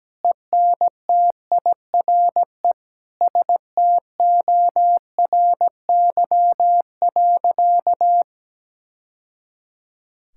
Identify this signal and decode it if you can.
Morse code, heard as a keyed tone, about 17 wpm, 700 Hz. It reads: ENTIRE STORY.
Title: Sentences from Top 1000 Words - Review Part 5 - 17wpm